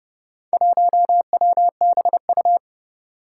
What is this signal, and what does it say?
Morse code, heard as a keyed tone, about 30 wpm, 700 Hz. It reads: 1WBU
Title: Four Challenging Letter-Number Combinations 30wpm